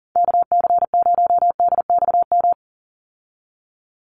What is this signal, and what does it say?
Morse code, heard as a keyed tone, about 39 wpm, 700 Hz. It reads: KC0DXM
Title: USA Call Signs 40wpm